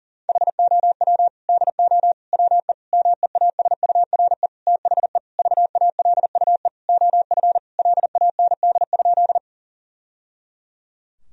Morse code, heard as a keyed tone, about 40 wpm, 700 Hz. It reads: HOW DO WE MEASURE THE VALUE OF LAND?
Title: Sentences from Top 1000 Words - Review Part 3 - 40wpm